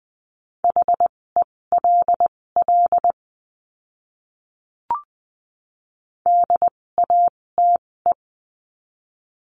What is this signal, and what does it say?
Morse code, heard as a keyed tone, about 20 wpm, spaced as for 16 wpm, 700 Hz. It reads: HELL DATE
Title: 256 Lesson 42a - Words 12-20wpm